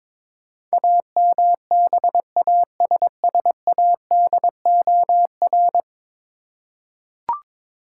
Morse code, heard as a keyed tone, about 22 wpm, 700 Hz. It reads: AMBASSADOR
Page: morse-code-ninja-practice-22wpm